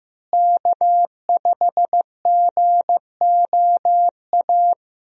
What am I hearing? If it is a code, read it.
Morse code, heard as a keyed tone, about 15 wpm, 700 Hz. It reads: K5GOA
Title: USA Call Signs - 5 Characters - Repeated 3x - 15wpm